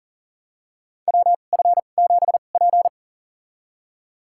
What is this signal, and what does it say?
Morse code, heard as a keyed tone, about 40 wpm, spaced as for 29 wpm, 690 Hz. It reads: WF7P
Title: USA Call Signs - 4 Characters - 2x Character Spacing - 40wpm